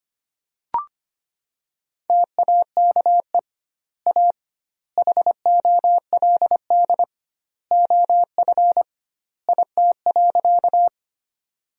Morse code, heard as a keyed tone, about 25 wpm, 700 Hz. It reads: TAKE A HOLD OF IT.
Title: 2x Word-Spacing - Sentences from Top 400 Words - 25wpm